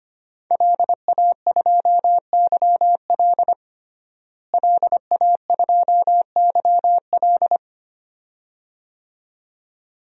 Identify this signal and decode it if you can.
Morse code, heard as a keyed tone, about 25 wpm, 700 Hz. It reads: LA2YL LA2YL
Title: DXCC-100 Call Signs - 5 Characters - Repeated 2x - 25wpm